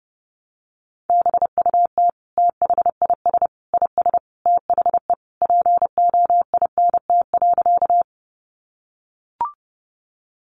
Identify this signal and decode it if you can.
Morse code, heard as a keyed tone, about 30 wpm, 700 Hz. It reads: BUT THIS IS THE POINT.
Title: Sentences from Top 200 Words 30wpm